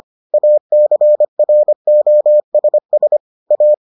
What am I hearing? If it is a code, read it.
Morse code, heard as a keyed tone, about 25 wpm, 600 Hz. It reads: ACROSS A